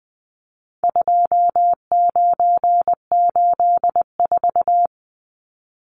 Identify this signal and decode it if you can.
Morse code, heard as a keyed tone, about 20 wpm, 700 Hz. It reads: 2984
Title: Numbers - 4-Digit 20wpm